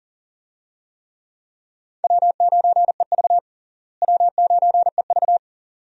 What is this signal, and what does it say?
Morse code, heard as a keyed tone, about 39 wpm, 700 Hz. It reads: W9EV W9EV
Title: USA Call Signs - 4 Characters - Repeated 2x - 40wpm